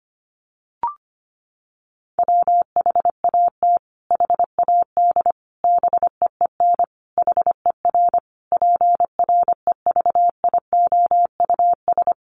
Morse code, heard as a keyed tone, about 25 wpm, 700 Hz. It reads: WHAT HAD BEEN HER PREVIOUS
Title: Sentences from Top 1500 Words 25wpm